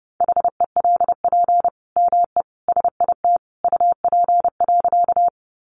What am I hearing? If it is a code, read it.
Morse code, heard as a keyed tone, about 30 wpm, 700 Hz. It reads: HELP ME SIT UP.
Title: Sentences from Top 500 Words 30wpm